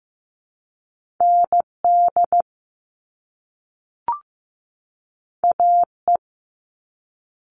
Morse code, heard as a keyed tone, about 15 wpm, 700 Hz. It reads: ND AE
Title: Top Two Letter Combinations in Words 15wpm